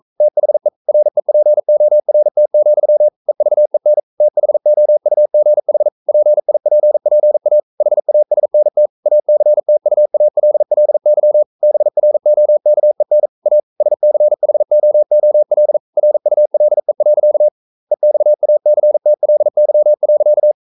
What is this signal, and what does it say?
Morse code, heard as a keyed tone, about 42 wpm, 600 Hz. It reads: THE REPORT, EVEN THOUGH PIPPA HASNT ACTUALLY BROKEN A SCHOOL RULE. EXACTLY.
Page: Romanen Never af Ken Follet på engelsk.